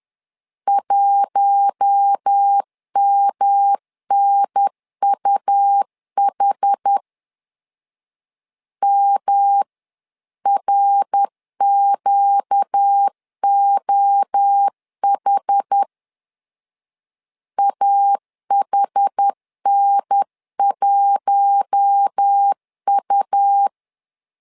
Morse code, heard as a keyed tone, about 10 wpm, 800 Hz. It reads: 1MNUH MRQOH AHN1U
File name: LEZIONI / 4 Q1 10wpm 2448